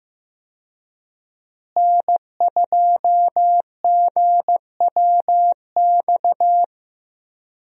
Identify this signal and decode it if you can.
Morse code, heard as a keyed tone, about 15 wpm, 700 Hz. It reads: N2GWX
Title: USA Call Signs - 5 Characters - Repeated 2x - 15wpm